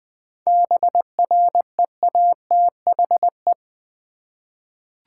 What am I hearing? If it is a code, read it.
Morse code, heard as a keyed tone, about 20 wpm, 700 Hz. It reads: BREATHE
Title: 7 Letter Words - First 3 Letters Given - 20wpm